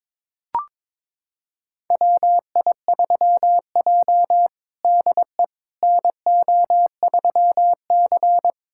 Morse code, heard as a keyed tone, about 22 wpm, 700 Hz. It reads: WI3J DE NO3C
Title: USA Callsign DE Callsign - 4 Characters - 22wpm